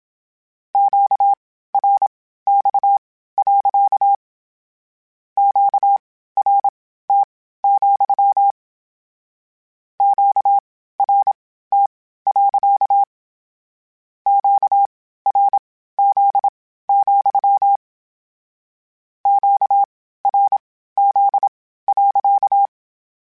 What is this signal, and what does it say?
Morse code, heard as a keyed tone, about 27 wpm, spaced as for 15 wpm, 800 Hz. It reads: QRX. QRT, QRT. QRZ, QRZ.
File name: Q Calls / Q Calls-mp3s / Q Calls sppc @25